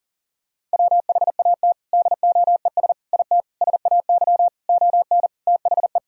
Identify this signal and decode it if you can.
Morse code, heard as a keyed tone, about 40 wpm, 700 Hz. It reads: WHAT DOES IT SAY ON THE
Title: Sentences from Top 1000 Words - Review Part 2 - 40wpm